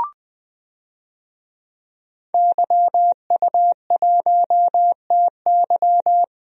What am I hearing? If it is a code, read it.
Morse code, heard as a keyed tone, about 20 wpm, 700 Hz. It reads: YU1TY
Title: DXCC-100 Call Signs - 5 Characters - Repeated 2x - 20wpm